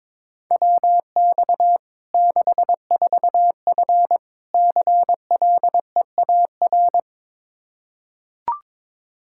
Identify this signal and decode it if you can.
Morse code, heard as a keyed tone, about 22 wpm, 700 Hz. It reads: WX 64F CLEAR
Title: Practice QSO - Exchange 2 - Info - 22wpm